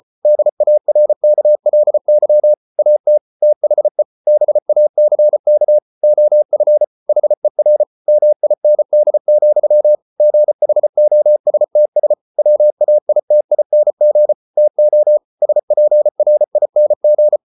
Morse code, heard as a keyed tone, about 34 wpm, 600 Hz. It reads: DARKLY AT THE BACK OF HER MIND, GHOSTS WAITING TO SPRING